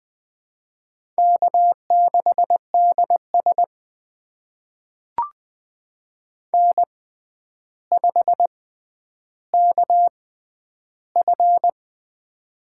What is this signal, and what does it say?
Morse code, heard as a keyed tone, about 20 wpm, spaced as for 10 wpm, 700 Hz. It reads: K6DS N5KF
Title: USA Call Signs - 4 Characters - 6x Character Spacing - 20wpm